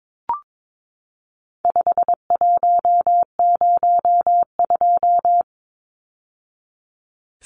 Morse code, heard as a keyed tone, about 22 wpm, 700 Hz. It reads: 5102
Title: Numbers - 4-Digit 22wpm